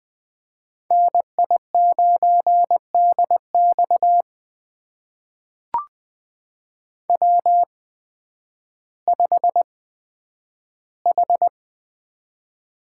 Morse code, heard as a keyed tone, about 20 wpm, spaced as for 11 wpm, 700 Hz. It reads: NI9DX W5H
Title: USA Call Signs - 5 Characters - 8x Character Spacing - 20wpm